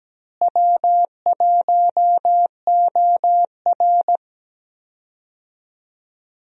Morse code, heard as a keyed tone, about 17 wpm, 700 Hz. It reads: W1OR
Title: USA Call Signs - 4 Characters - Repeated 3x - 17wpm